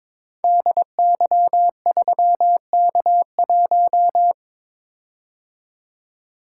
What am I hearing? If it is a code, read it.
Morse code, heard as a keyed tone, about 22 wpm, 700 Hz. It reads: DY3K1
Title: Code Groups - 5 Letter:Numbers - 22wpm